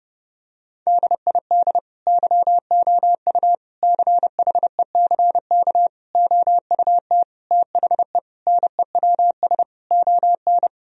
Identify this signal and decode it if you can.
Morse code, heard as a keyed tone, about 30 wpm, 700 Hz. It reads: DID YOU CHECK OUT THE NEWS ON
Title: Sentences from Top 1000 Words - Review Part 4 - 30wpm